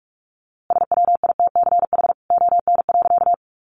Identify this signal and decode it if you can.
Morse code, heard as a keyed tone, about 44 wpm, 700 Hz. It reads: SWITCH ON.
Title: A Sound Of Thunder 45wpm